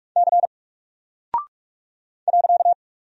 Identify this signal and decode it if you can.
Morse code, heard as a keyed tone, about 44 wpm, 700 Hz. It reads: C .
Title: Single Character (Letter:Number:Punctuation:Prosign) 45wpm